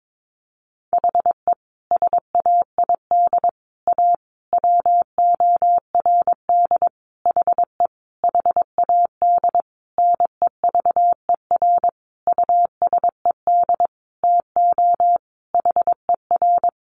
Morse code, heard as a keyed tone, about 22 wpm, 700 Hz. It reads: HE SAID A WORD HE HAD NEVER USED TO HER
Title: Sentences from Top 600 Words 22wpm